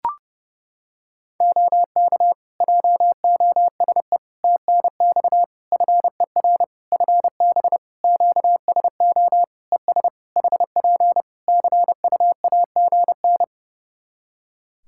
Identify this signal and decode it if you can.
Morse code, heard as a keyed tone, about 30 wpm, 700 Hz. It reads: OK JOSE TNX FER FB QSO ES HP CUAGN
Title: K1USN SST QSO Practice 30wpm